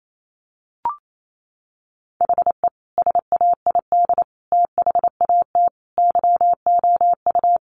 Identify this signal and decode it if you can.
Morse code, heard as a keyed tone, about 28 wpm, 700 Hz. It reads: HE SAID THAT YOU
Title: Sentences from Top 1000 Words 28wpm